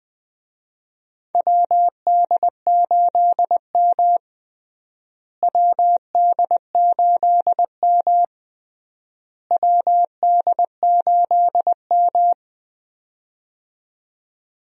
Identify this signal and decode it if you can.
Morse code, heard as a keyed tone, about 20 wpm, 700 Hz. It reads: WD8M WD8M WD8M
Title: USA Call Signs - 4 Characters - Repeated 3x - 20wpm